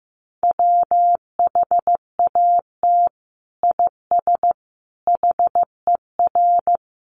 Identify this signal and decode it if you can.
Morse code, heard as a keyed tone, about 15 wpm, 700 Hz. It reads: WHAT IS HER
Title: Sentences from Top 1500 Words 15wpm